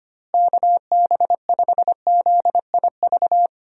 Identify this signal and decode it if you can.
Morse code, heard as a keyed tone, about 25 wpm, 700 Hz. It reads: KB5ZIV
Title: USA Call Signs - 6 Characters - Repeated 3x - 25wpm